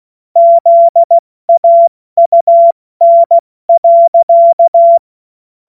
Morse code, heard as a keyed tone, about 16 wpm, 680 Hz. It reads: ZAUN.
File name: Morsekurs / Witze / Witz 26 Dezember 2023 70 F80